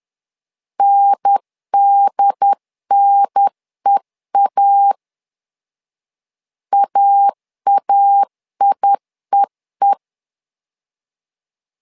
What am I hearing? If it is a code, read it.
Morse code, heard as a keyed tone, about 10 wpm, 800 Hz. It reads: NDNEA AAIEE
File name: LEZIONI / 2 UHDN / 2 UHDN 10wpm 1904